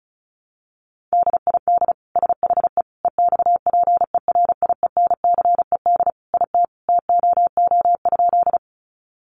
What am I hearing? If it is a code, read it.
Morse code, heard as a keyed tone, about 34 wpm, 700 Hz. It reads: DID SHE EXPERIENCED IT TOO?
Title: Sentences from Top 1100 Words 35wpm